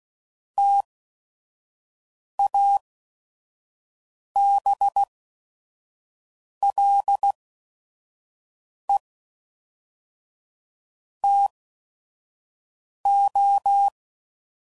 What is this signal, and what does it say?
Morse code, heard as a keyed tone, about 16 wpm, spaced as for 5 wpm, 800 Hz. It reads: TABLE TO
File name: CW Code Groups @5WPM / Technical Subjects / Gates Intro